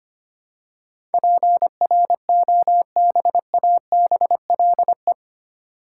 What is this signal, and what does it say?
Morse code, heard as a keyed tone, about 25 wpm, 700 Hz. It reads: PROBABLE
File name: 8 Letter Words - First 3 Letters Given - 25wpm